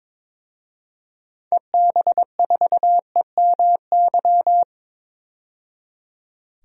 Morse code, heard as a keyed tone, about 22 wpm, 700 Hz. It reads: EB4EMY
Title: DXCC-100 Call Signs - 6 Characters - 22wpm